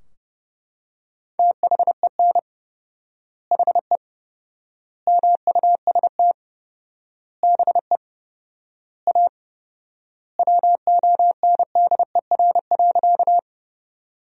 Morse code, heard as a keyed tone, about 31 wpm, 700 Hz. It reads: THEN HE MUST BE A WONDER.
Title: 4x Word-Spacing - Sentences from Top 500 Words - 30wpm